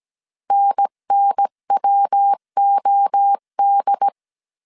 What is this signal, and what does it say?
Morse code, heard as a keyed tone, about 17 wpm, 800 Hz. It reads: NNWOD